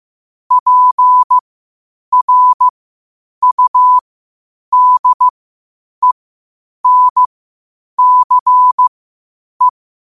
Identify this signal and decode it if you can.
Morse code, heard as a keyed tone, about 15 wpm, spaced as for 8 wpm, 1000 Hz. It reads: PRUDENCE
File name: qotdfiles / 8wpm